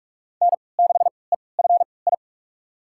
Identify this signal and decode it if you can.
Morse code, heard as a keyed tone, about 46 wpm, spaced as for 26 wpm, 700 Hz. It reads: N6EFI
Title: USA Call Signs - 5 Characters - 3x Character Spacing - 45wpm